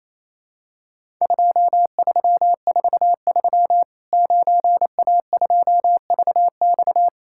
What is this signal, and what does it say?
Morse code, heard as a keyed tone, about 28 wpm, 700 Hz. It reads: 2343 9A2VX
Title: Worked All Europe DX Contest - QTC Exchange - 28wpm